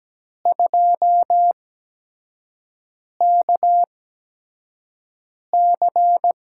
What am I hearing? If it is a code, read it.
Morse code, heard as a keyed tone, about 17 wpm, spaced as for 5 wpm, 700 Hz. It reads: 2KC
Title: USA Call Signs - 5 Characters - 8x Character Spacing - 17wpm